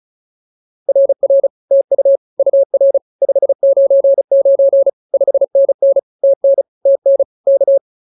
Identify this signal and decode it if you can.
Morse code, heard as a keyed tone, about 34 wpm, 550 Hz. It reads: RR TU UR 599 5NN TN TN K